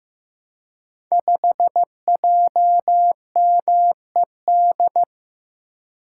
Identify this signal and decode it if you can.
Morse code, heard as a keyed tone, about 15 wpm, 700 Hz. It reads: 5JMED
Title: Code Groups - 5 Letter:Numbers - 15wpm